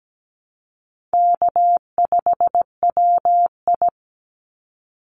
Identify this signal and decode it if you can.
Morse code, heard as a keyed tone, about 17 wpm, 700 Hz. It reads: K5WI
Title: NA SOTA Chaser Callsigns 17wpm